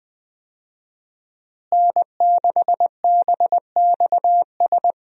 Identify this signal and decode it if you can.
Morse code, heard as a keyed tone, about 20 wpm, 700 Hz. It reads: N6BXS
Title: USA Call Signs - 5 Characters - Repeated 3x - 20wpm